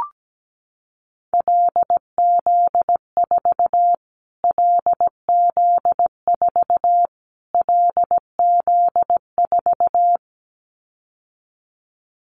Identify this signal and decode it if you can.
Morse code, heard as a keyed tone, about 17 wpm, 700 Hz. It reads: LZ4 LZ4 LZ4
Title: Callsign Prefixes for Top 25 DXCC - Repeated 3x - 17wpm